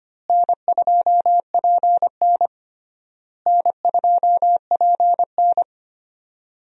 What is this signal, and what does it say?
Morse code, heard as a keyed tone, about 25 wpm, 700 Hz. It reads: N2PN N2PN
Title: USA Call Signs - 4 Characters - Repeated 3x - 25wpm